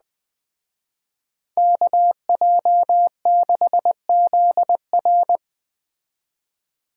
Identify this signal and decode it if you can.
Morse code, heard as a keyed tone, about 20 wpm, 700 Hz. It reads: KJ6ZR